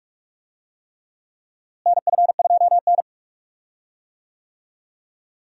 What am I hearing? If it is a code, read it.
Morse code, heard as a keyed tone, about 47 wpm, 700 Hz. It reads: NF2N
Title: USA Call Signs - 4 Characters - Repeated 3x - 45wpm